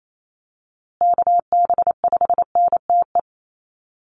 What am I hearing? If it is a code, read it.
Morse code, heard as a keyed tone, about 28 wpm, 700 Hz. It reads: KB5NTE
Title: USA Call Signs Repeated 3x 28wpm